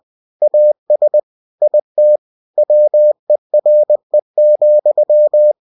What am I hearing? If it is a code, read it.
Morse code, heard as a keyed tone, about 20 wpm, 600 Hz. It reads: AS IT WERE,
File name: cw-books / winnie-the-pooh / winnie-20wpm-page-0023